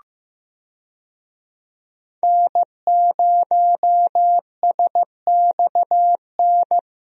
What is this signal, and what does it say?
Morse code, heard as a keyed tone, about 15 wpm, 700 Hz. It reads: N0SXN
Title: USA Call Signs - 5 Characters - Repeated 2x - 15wpm